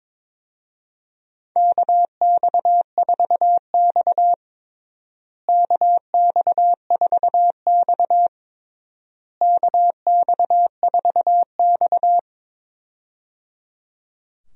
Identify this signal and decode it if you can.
Morse code, heard as a keyed tone, about 21 wpm, 700 Hz. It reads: KX4X KX4X KX4X
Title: USA Call Signs - 4 Characters - Repeated 3x - 22wpm